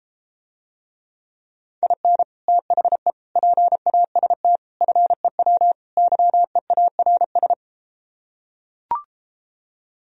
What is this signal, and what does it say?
Morse code, heard as a keyed tone, about 33 wpm, 700 Hz. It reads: IN THE PAST FEW YEARS